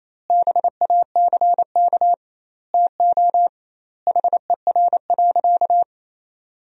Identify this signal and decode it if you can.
Morse code, heard as a keyed tone, about 28 wpm, 700 Hz. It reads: BACK TO HER.
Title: 2x Word-Spacing - Sentences from Top 300 Words - 28wpm